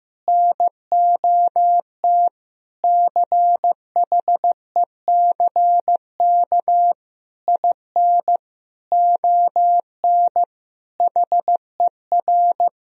Morse code, heard as a keyed tone, about 15 wpm, 700 Hz. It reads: NOT CHECK IN ON HER
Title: Sentences from Top 1000 Words - Review Part 3 - 15wpm